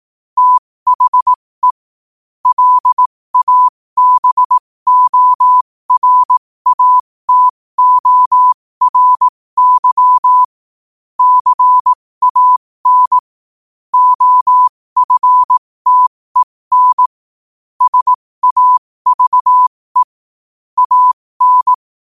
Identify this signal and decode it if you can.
Morse code, heard as a keyed tone, about 18 wpm, spaced as for 15 wpm, 1000 Hz. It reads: THE LABORATORY CAN OFTEN SAVE AN
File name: qotdfiles / 15wpm